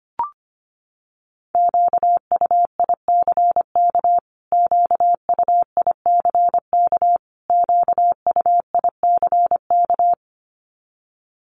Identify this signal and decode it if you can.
Morse code, heard as a keyed tone, about 25 wpm, 700 Hz. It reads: QUICK QUICK QUICK
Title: Top 500 Words Repeated 25wpm